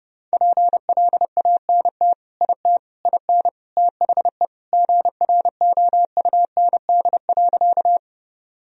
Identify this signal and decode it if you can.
Morse code, heard as a keyed tone, about 30 wpm, 700 Hz. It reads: PLANT IT IN THE GROUND.
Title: Sentences from Top 1000 Words - Review Part 1 - 30wpm